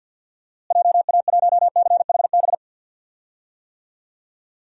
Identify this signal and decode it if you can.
Morse code, heard as a keyed tone, about 53 wpm, 700 Hz. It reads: JA1CHB
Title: DXCC-100 Call Signs - 6 Characters - Repeated 3x - 50wpm